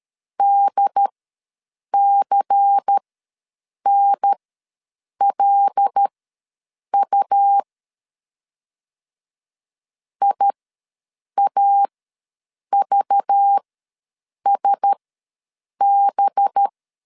Morse code, heard as a keyed tone, about 13 wpm, spaced as for 7 wpm, 800 Hz. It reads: DCNLU IAVSB
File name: LEZIONI / 6 PKX0 / 6 PK ws 12wpm 3116